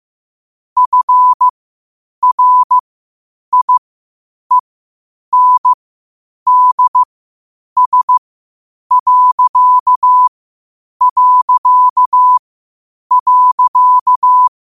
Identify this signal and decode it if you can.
Morse code, heard as a keyed tone, about 15 wpm, spaced as for 8 wpm, 1000 Hz. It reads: FRIENDS...
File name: qotdfiles / 8wpm